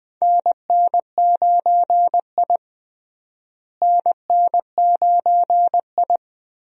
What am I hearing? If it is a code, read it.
Morse code, heard as a keyed tone, about 20 wpm, 700 Hz. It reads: NN9I NN9I
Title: USA Call Signs - 4 Characters - Repeated 2x - 20wpm